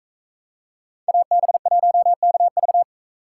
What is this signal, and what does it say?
Morse code, heard as a keyed tone, about 44 wpm, 700 Hz. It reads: AB1KV